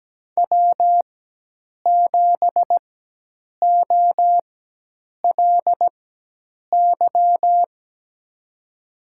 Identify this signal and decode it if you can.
Morse code, heard as a keyed tone, about 17 wpm, spaced as for 8 wpm, 700 Hz. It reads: W7OLY
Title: USA Call Signs - 4x Character Spacing - 17wpm